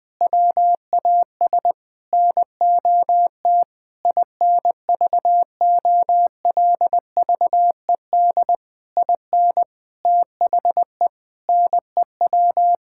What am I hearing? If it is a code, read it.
Morse code, heard as a keyed tone, about 19 wpm, 700 Hz. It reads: WAS NOT INVOLVED IN THE NEW
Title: Sentences with 8-Word Phrases - 20wpm